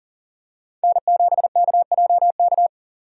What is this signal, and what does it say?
Morse code, heard as a keyed tone, about 40 wpm, 700 Hz. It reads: N7KJK
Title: USA Call Signs - 5 Characters - 8x Character Spacing - 40wpm